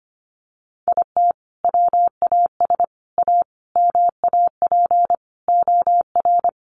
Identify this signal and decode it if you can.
Morse code, heard as a keyed tone, about 25 wpm, 700 Hz. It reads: IT WAS A MAP OR
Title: Sentences from Top 1300 Words 25wpm